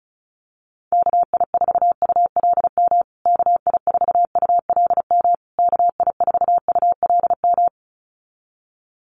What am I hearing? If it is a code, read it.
Morse code, heard as a keyed tone, about 34 wpm, 700 Hz. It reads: KI4ULM KI4ULM KI4ULM
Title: USA Call Signs Repeated 3x 35wpm